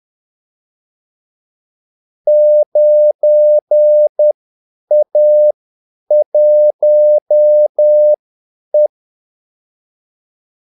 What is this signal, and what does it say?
Morse code, heard as a keyed tone, about 10 wpm, spaced as for 8 wpm, 600 Hz. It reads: 9A1E
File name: Callsigns-10-8wpm